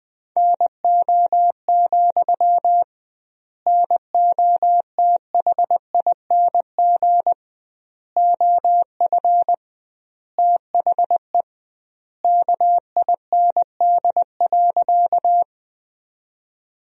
Morse code, heard as a keyed tone, about 21 wpm, 700 Hz. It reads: NO, NOTHING OF THE KIND.
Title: 2x Word-Spacing - Sentences from Top 500 Words - 20wpm